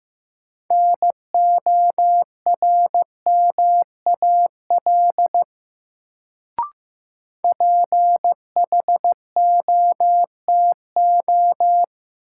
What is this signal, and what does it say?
Morse code, heard as a keyed tone, about 15 wpm, 700 Hz. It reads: NORMAL PHOTO
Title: Sentences from Top 1200 Words - New Word Drill - 15wpm